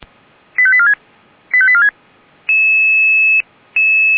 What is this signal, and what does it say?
Five-tone selective call (ZVEI), heard as quick runs of five tones, tones of about 70 ms. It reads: 85846 85846
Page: /464/SDR/ZVEI/